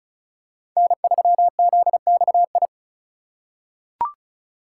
Morse code, heard as a keyed tone, about 36 wpm, 700 Hz. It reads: N3ZXI
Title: USA Call Signs - 5 Characters - 2x Character Spacing - 35wpm